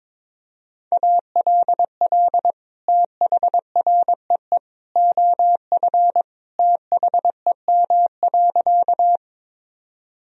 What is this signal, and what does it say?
Morse code, heard as a keyed tone, about 22 wpm, 700 Hz. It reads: ALL THREE OF THEM.
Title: Sentences from Top 1000 Words - Review Part 1 - 22wpm